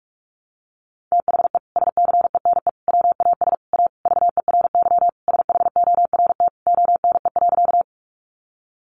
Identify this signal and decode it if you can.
Morse code, heard as a keyed tone, about 44 wpm, 700 Hz. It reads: THE SCENE WAS A VERY SHORT ONE.
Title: Sentences from Top 1100 Words 45wpm